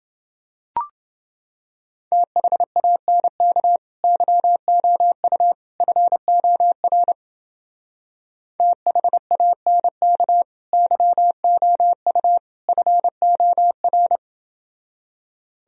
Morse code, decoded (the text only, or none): THANK YOU FOR THANK YOU FOR